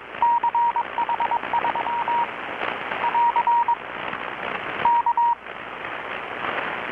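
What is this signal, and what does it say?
Morse code, heard as a keyed tone, about 22 wpm, 940 Hz. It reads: CH3 <AR> K
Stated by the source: Morse from KPH's Night of Nights XX maritime radiotelegraphy event, received in Chiapas, Mexico about 3,800 km away on a Kenwood R-600 with a grounded 10 m long-wire antenna.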